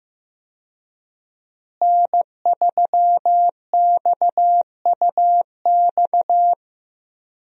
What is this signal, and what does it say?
Morse code, heard as a keyed tone, about 15 wpm, 700 Hz. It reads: N3XUX
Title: USA Call Signs - 5 Characters - Repeated 2x - 15wpm